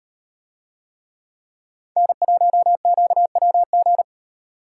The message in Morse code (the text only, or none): N1QWG